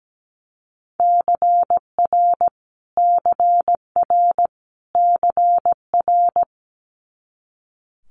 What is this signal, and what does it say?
Morse code, heard as a keyed tone, about 17 wpm, 700 Hz. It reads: CR CR CR
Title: Multi-letter Phonemes Repeated 3x 17wpm